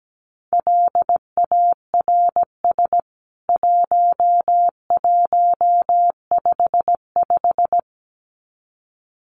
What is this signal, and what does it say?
Morse code, heard as a keyed tone, about 17 wpm, 700 Hz. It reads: LARS 1155
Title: CWT Exchange 17wpm